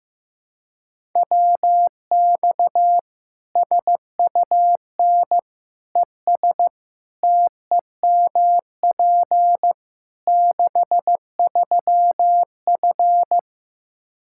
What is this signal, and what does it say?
Morse code, heard as a keyed tone, about 15 wpm, 700 Hz. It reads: WX SUN ES TEMP 63F
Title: Practice QSO - Exchange 2 - Info - 15wpm